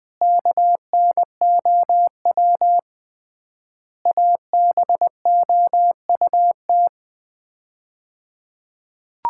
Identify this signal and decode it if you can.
Morse code, heard as a keyed tone, about 20 wpm, 700 Hz. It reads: KNOW ABOUT
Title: 3x Word-Spacing - Sets of 5 Words - 20wpm